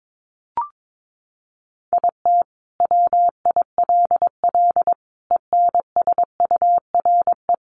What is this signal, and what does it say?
Morse code, heard as a keyed tone, about 22 wpm, 700 Hz. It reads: IT WILL ENSURE
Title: Sentences from Top 1100 Words 22wpm